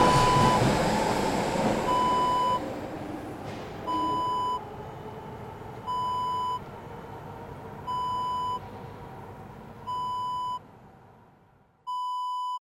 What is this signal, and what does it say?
There is no code.